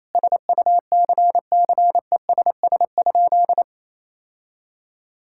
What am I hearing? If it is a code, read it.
Morse code, heard as a keyed tone, about 28 wpm, 690 Hz. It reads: SUCCESS?
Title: Sentences from Top 1000 Words - Review Part 6 - 28wpm